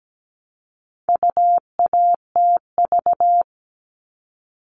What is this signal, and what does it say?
Morse code, heard as a keyed tone, about 17 wpm, 700 Hz. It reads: UATV
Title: All Cut Numbers - 4-Digit 17wpm